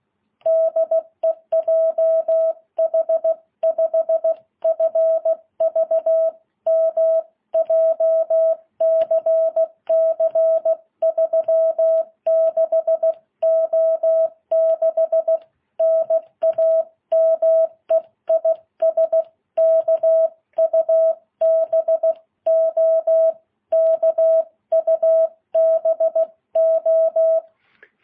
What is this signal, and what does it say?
Morse code, heard as a keyed tone, about 16 wpm, 660 Hz. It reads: DE JH5FVM JCC 36O6 NAME IS KUBO KUBO
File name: jh8cbh